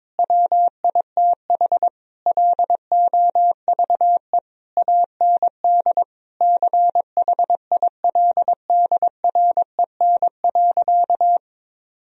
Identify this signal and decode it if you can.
Morse code, heard as a keyed tone, about 22 wpm, 700 Hz. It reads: WITH LOVE AND CHILDREN.
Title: Sentences from Top 1000 Words - Review Part 3 - 22wpm